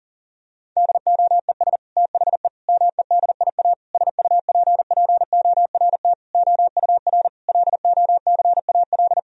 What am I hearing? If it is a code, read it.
Morse code, heard as a keyed tone, about 40 wpm, 700 Hz. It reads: DOES THE MEDIA SUPPORT OUR LOCAL
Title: Sentences from Top 1000 Words - Review Part 4 - 40wpm